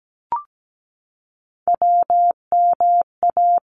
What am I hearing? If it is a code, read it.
Morse code, heard as a keyed tone, about 17 wpm, 700 Hz. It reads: WMA
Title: ARRL Sections 17wpm